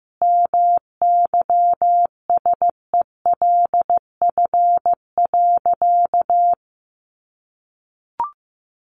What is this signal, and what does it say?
Morse code, heard as a keyed tone, about 15 wpm, 700 Hz. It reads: MYSELF.
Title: Sentences from Top 1300 Words 15wpm